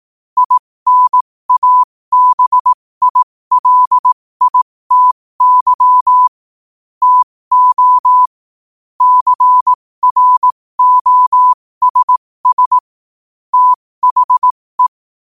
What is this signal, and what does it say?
Morse code, heard as a keyed tone, about 18 wpm, spaced as for 15 wpm, 1000 Hz. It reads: INABILITY TO CROSS THE
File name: qotdfiles / 15wpm